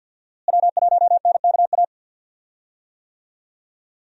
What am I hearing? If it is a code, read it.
Morse code, heard as a keyed tone, about 48 wpm, 700 Hz. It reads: W1NKA
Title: USA Call Signs - 5 Characters - Repeated 3x - 50wpm